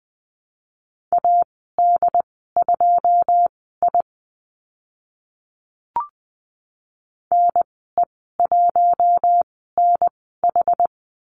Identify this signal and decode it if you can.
Morse code, heard as a keyed tone, about 20 wpm, spaced as for 15 wpm, 700 Hz. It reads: AD2I NE1NH
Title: 082 Lesson 14g - Number 2 - Call Signs 20wpm